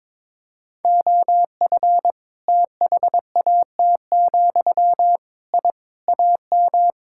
Morse code, heard as a keyed tone, about 22 wpm, 700 Hz. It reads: OF THAT, I AM
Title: Sentences from Top 1000 Words - Review Part 5 - 22wpm